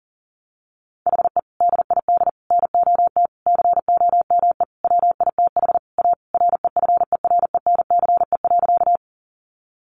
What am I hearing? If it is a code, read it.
Morse code, heard as a keyed tone, about 37 wpm, 700 Hz. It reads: HE DID NOT COME WITH A REFERENCE.
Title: Sentences from Top 1200 Words 40wpm